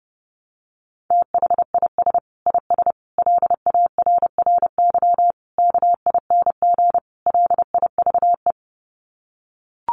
Morse code, heard as a keyed tone, about 30 wpm, 700 Hz. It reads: THIS IS LARRY KING LIVE
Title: Sets of 5 Words Encore2 30wpm